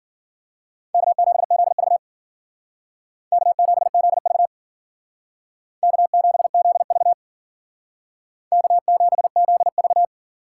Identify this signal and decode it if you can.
Morse code, heard as a keyed tone, about 49 wpm, 700 Hz. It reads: K7ZV K7ZV K7ZV K7ZV
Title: Speed Racing - USA 1x2 Call Signs 40wpm